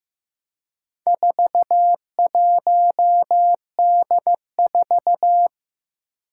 Morse code, heard as a keyed tone, about 15 wpm, 700 Hz. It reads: 41D4
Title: Four Challenging Letter-Number Combinations 15wpm